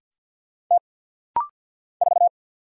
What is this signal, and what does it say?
Morse code, heard as a keyed tone, about 50 wpm, 700 Hz. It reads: T 4
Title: Single Letter-Number Alt - Rapid-Fire 50wpm